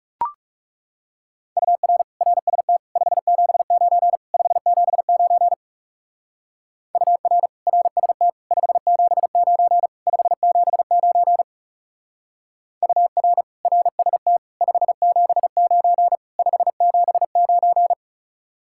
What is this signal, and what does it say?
Morse code, heard as a keyed tone, about 38 wpm, 700 Hz. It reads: UR RST 579 579 UR RST 579 579 UR RST 579 579